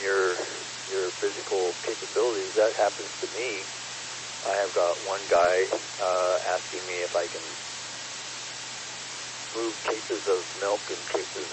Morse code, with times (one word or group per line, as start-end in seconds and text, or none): none